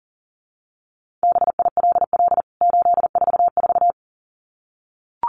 Morse code, heard as a keyed tone, about 40 wpm, 700 Hz. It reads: BILL 844
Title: CWT Exchange 40wpm